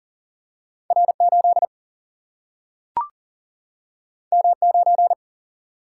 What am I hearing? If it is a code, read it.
Morse code, heard as a keyed tone, about 39 wpm, 700 Hz. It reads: R8 M9
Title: Single-Letter Single-Number 40wpm